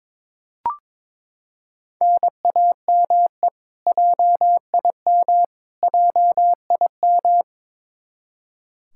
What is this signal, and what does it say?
Morse code, heard as a keyed tone, about 22 wpm, 700 Hz. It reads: NAME JIM JIM
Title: Practice QSO - Exchange 1 - Beginning - 22wpm